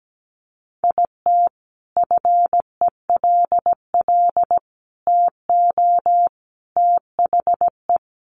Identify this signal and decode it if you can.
Morse code, heard as a keyed tone, about 17 wpm, 700 Hz. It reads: IT FELL TO THE
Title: A Sound Of Thunder 17wpm